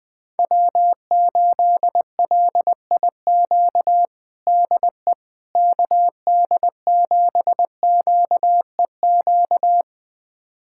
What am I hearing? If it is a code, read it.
Morse code, heard as a keyed tone, about 21 wpm, 700 Hz. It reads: W8LIQ DE KD7QEQ
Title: USA Callsign DE Callsign 20wpm